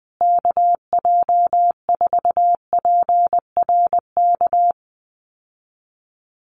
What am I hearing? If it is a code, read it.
Morse code, heard as a keyed tone, about 19 wpm, 700 Hz. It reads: KJ4PRK
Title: USA Call Signs Repeated 3x 20wpm